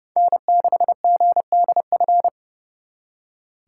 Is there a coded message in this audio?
Yes: Morse code from 0.2 to 2.3 s, heard as a keyed tone, about 38 wpm, 700 Hz. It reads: N6GDF